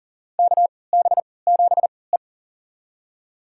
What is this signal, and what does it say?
Morse code, heard as a keyed tone, about 40 wpm, spaced as for 23 wpm, 700 Hz. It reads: KB7E
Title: USA Call Signs - 3x Character Spacing - 40wpm